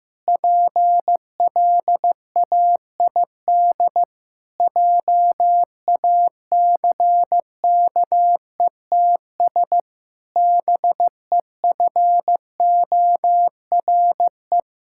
Morse code, heard as a keyed tone, about 15 wpm, 700 Hz. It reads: PLAID JACKETS BEFORE
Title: Pangram Challenge v2 - 15wpm